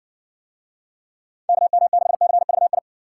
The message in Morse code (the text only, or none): KM6CVI